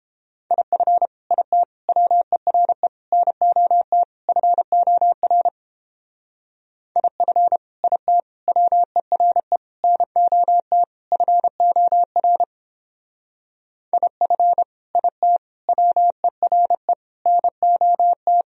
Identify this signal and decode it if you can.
Morse code, heard as a keyed tone, about 31 wpm, 700 Hz. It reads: IF IT WERE NOT FOR IF IT WERE NOT FOR IF IT WERE NOT